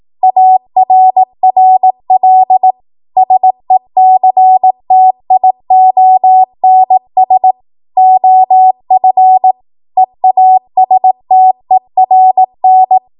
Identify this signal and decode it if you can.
Morse code, heard as a keyed tone, about 18 wpm, 750 Hz. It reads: ARRL SECTIONS OF EASTERN
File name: Archive / 18 WPM / 18WPM